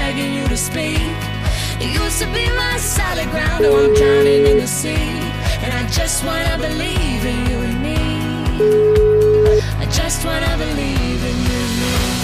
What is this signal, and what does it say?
Ringback tone (Europe, 425 Hz), with noise.